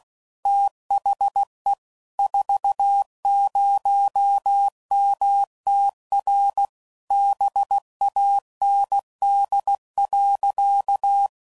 Morse code, heard as a keyed tone, about 16 wpm, 800 Hz. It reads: THE 40MTR BAND.
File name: CW Code Groups @15WPM / QSOs / kc7haa